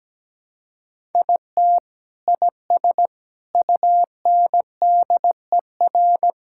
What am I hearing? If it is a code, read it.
Morse code, heard as a keyed tone, about 17 wpm, 700 Hz. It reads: IT IS UNDER
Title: Sentences from Top 1000 Words - Review Part 5 - 17wpm